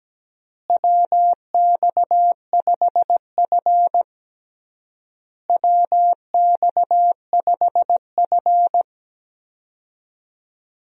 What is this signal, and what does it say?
Morse code, heard as a keyed tone, about 17 wpm, 700 Hz. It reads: WX5F WX5F
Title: USA Call Signs - 4 Characters - Repeated 3x - 17wpm